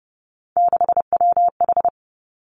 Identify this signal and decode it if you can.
Morse code, heard as a keyed tone, about 30 wpm, 700 Hz. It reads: 6WH
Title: Three Challenging Letter-Number Combinations 30wpm